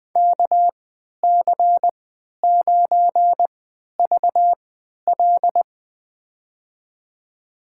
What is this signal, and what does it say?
Morse code, heard as a keyed tone, about 20 wpm, spaced as for 11 wpm, 700 Hz. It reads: KC9VL
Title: USA Call Signs - 5 Characters - 3x Character Spacing - 20wpm